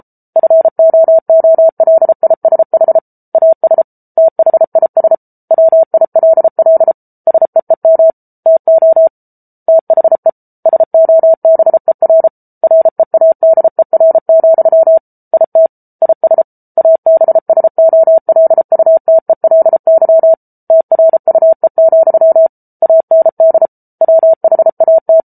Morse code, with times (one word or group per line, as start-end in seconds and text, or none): 0.4-3.0 FOOLISH
3.4-3.8 AS
4.2-5.2 THIS
5.5-6.9 WILL
7.3-8.1 SEEM
8.5-9.1 TO
9.7-10.3 THE
10.7-12.3 SOBER
12.6-15.0 READER,
15.3-15.7 IT
16.0-16.4 IS
16.8-20.3 ABSOLUTELY
20.7-22.5 TRUE,
22.8-23.7 AND
24.0-25.2 WHAT